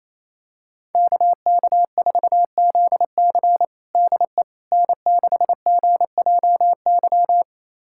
Morse code, heard as a keyed tone, about 28 wpm, 700 Hz. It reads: KK4ZC DE N6GJY
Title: USA Callsign DE Callsign - 5 Characters - 28wpm